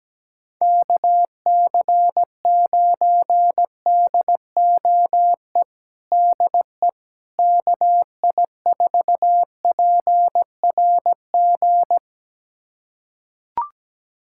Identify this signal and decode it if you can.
Morse code, heard as a keyed tone, about 17 wpm, 700 Hz. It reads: KC9DOE DE KI4PRG
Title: USA Callsign DE Callsign - 6 Characters - 17wpm